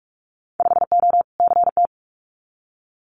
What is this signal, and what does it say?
Morse code, heard as a keyed tone, about 46 wpm, 700 Hz. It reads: 5O CT
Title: Winter Field Day Exchanges Encore 45wpm